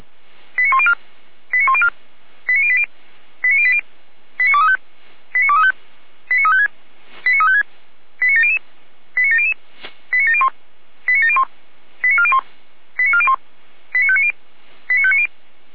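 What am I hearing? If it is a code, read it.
Five-tone selective call (ZVEI), heard as quick runs of five tones, tones of about 70 ms. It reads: 89104 89104 89080 89080 89235 89235 89356 89356 89700 89700 89701 89701 89401 89401 89590 89590
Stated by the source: ZVEI five-tone call received on SDR.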